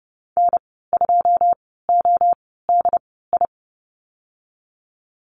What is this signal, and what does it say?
Morse code, heard as a keyed tone, about 30 wpm, spaced as for 17 wpm, 700 Hz. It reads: N2ODI
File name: Lesson 14g - Number 2 - Call Signs 30wpm